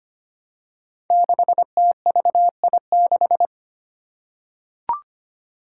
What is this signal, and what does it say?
Morse code, heard as a keyed tone, about 25 wpm, 700 Hz. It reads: 6TVI6
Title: Five Challenging Letter-Number Combinations 25wpm